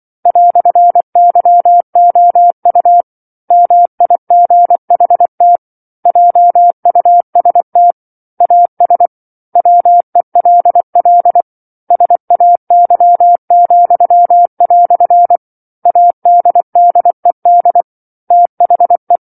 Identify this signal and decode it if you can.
Morse code, heard as a keyed tone, about 24 wpm, 700 Hz. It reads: "YOU MIGHT JUST AS WELL SAY," ADDED THE
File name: 07-Chapter-07 24 WPM